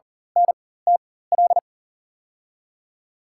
Morse code, heard as a keyed tone, about 40 wpm, spaced as for 19 wpm, 700 Hz. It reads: NTL
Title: USA Call Signs - 6 Characters - 4x Character Spacing - 40wpm